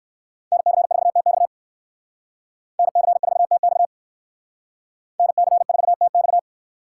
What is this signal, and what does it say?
Morse code, heard as a keyed tone, about 57 wpm, 700 Hz. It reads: NC4TX NC4TX NC4TX